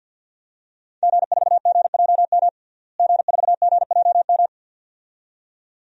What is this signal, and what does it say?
Morse code, heard as a keyed tone, about 52 wpm, 700 Hz. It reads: G4GJM G4GJM
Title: DXCC-100 Call Signs - 5 Characters - Repeated 2x - 50wpm